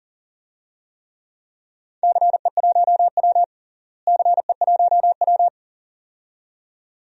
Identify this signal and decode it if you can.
Morse code, heard as a keyed tone, about 38 wpm, 700 Hz. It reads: CE1W CE1W
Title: DXCC-100 Call Signs - 4 Characters - Repeated 2x - 40wpm